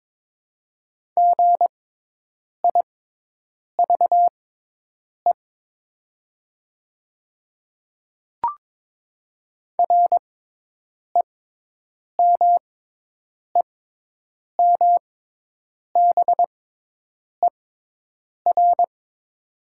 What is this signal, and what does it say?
Morse code, heard as a keyed tone, about 22 wpm, spaced as for 8 wpm, 690 Hz. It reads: GIVE REMEMBER
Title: 6x Character Spacing - Top 500 Words - 22wpm